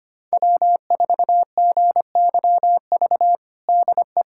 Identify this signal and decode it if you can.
Morse code, heard as a keyed tone, about 25 wpm, 700 Hz. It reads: W4GYV DE
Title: USA Callsign DE Callsign - 5 Characters - 25wpm